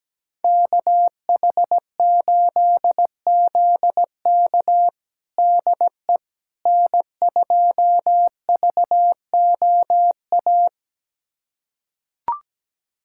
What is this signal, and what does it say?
Morse code, heard as a keyed tone, about 17 wpm, 700 Hz. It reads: KH8ZK DE N2VOA
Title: USA Callsign DE Callsign - 5 Characters - 17wpm